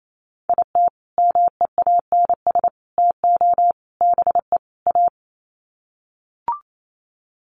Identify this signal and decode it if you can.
Morse code, heard as a keyed tone, about 28 wpm, 700 Hz. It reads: IT MEANS TO BE A
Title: Sets of 5 Words Encore2 28wpm